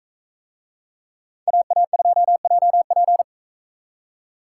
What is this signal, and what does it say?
Morse code, heard as a keyed tone, about 42 wpm, 700 Hz. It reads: AA2JP